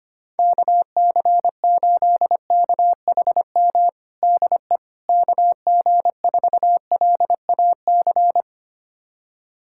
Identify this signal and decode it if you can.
Morse code, heard as a keyed tone, about 25 wpm, 700 Hz. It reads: KC8KHM DE KG4LAC
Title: USA Callsign DE Callsign - 6 Characters - 25wpm